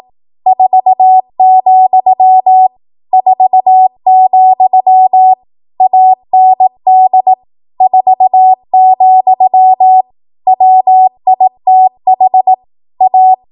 Morse code, heard as a keyed tone, about 18 wpm, 750 Hz. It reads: 4, 4, AND 4, WITH A
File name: Morse / 18WPM